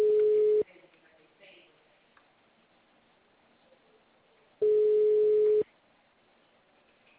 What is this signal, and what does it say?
Ringback tone (Europe, 425 Hz), with speech.